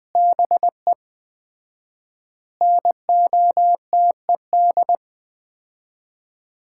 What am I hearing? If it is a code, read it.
Morse code, heard as a keyed tone, about 21 wpm, 700 Hz. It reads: BE NOTED
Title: 4x Word-Spacing - Sets of 5 Words - 20wpm